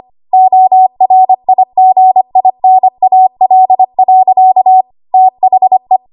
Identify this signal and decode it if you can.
Morse code, heard as a keyed tone, about 25 wpm, 750 Hz. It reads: ORIGINAL. THE
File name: Archive / 25 WPM / 25WPM